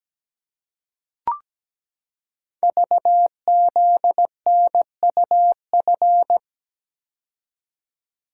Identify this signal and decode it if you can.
Morse code, heard as a keyed tone, about 17 wpm, 700 Hz. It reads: VZNUF
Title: Code Groups - 5 Letter:Numbers - 17wpm